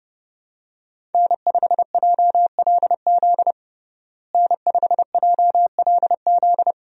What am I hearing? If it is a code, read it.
Morse code, heard as a keyed tone, about 29 wpm, 700 Hz. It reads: N5JLZ N5JLZ
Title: USA Call Signs - 5 Characters - Repeated 2x - 30wpm